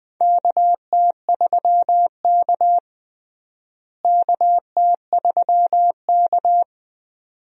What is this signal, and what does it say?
Morse code, heard as a keyed tone, about 20 wpm, 700 Hz. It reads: KT3K KT3K
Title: USA Call Signs - 4 Characters - Repeated 2x - 20wpm